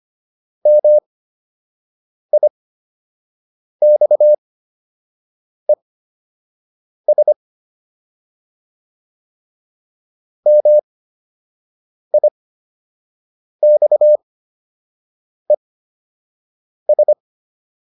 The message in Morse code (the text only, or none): MIXES MIXES